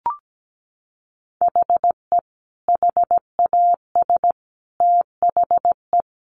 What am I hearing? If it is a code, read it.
Morse code, heard as a keyed tone, about 17 wpm, 700 Hz. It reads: HE HAS THE
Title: Sentences from Top 800 Words 17wpm